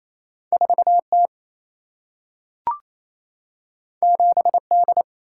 Morse code, heard as a keyed tone, about 28 wpm, 700 Hz. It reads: 4T 7D
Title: Single-Letter Single-Number 28wpm